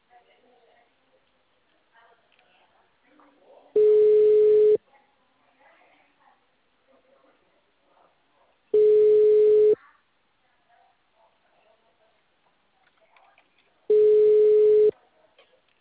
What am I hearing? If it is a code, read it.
Ringback tone (Europe, 425 Hz), with speech.